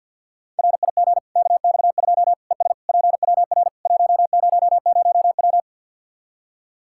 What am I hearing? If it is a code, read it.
Morse code, heard as a keyed tone, about 45 wpm, 700 Hz. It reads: RIG KX2 ES PWR 100W
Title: Practice QSO - Exchange 2 - Info - 50wpm